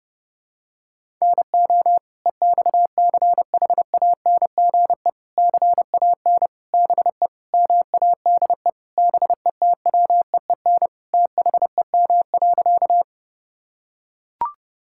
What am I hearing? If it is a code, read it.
Morse code, heard as a keyed tone, about 31 wpm, 700 Hz. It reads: NO EXCHANGE CAN BE MADE BETWEEN THEM.
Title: Sentences from Top 1000 Words - Review Part 6 - 30wpm